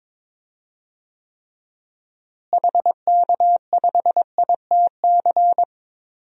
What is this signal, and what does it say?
Morse code, heard as a keyed tone, about 22 wpm, 700 Hz. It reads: HK5ITC
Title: DXCC-100 Call Signs - 6 Characters - Repeated 2x - 22wpm